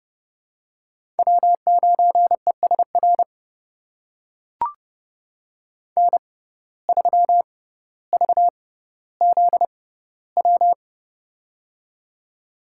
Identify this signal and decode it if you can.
Morse code, heard as a keyed tone, about 30 wpm, spaced as for 15 wpm, 700 Hz. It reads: W9ESR N3VZW
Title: USA Call Signs - 5 Characters - 6x Character Spacing - 30wpm